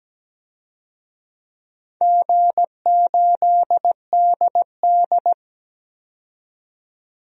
Morse code, heard as a keyed tone, about 17 wpm, 700 Hz. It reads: G8DD
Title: DXCC-100 Call Signs - 4 Characters - 17wpm